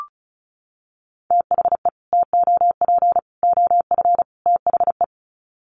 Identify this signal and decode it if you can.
Morse code, heard as a keyed tone, about 34 wpm, 700 Hz. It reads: THE TOP OF THE
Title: Sets of 4 Words 35wpm